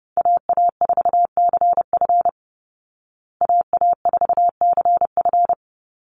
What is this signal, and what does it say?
Morse code, heard as a keyed tone, about 30 wpm, 700 Hz. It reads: AA4CF AA4CF
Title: Call Signs Repeated 2x - 30wpm